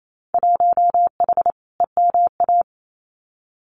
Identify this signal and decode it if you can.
Morse code, heard as a keyed tone, about 28 wpm, 700 Hz. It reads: 1H EMA
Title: Winter Field Day Exchanges 28wpm